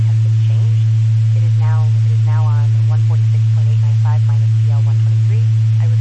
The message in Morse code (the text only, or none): none